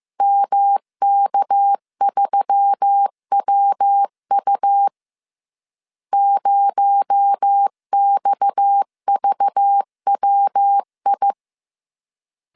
Morse code, heard as a keyed tone, about 15 wpm, 800 Hz. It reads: MK3WU 0XVWI